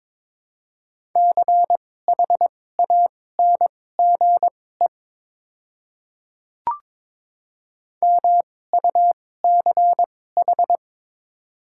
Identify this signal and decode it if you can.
Morse code, heard as a keyed tone, about 22 wpm, spaced as for 16 wpm, 700 Hz. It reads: CHANGE MUCH
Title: 2x Character Spacing - Top 200 Words - 22wpm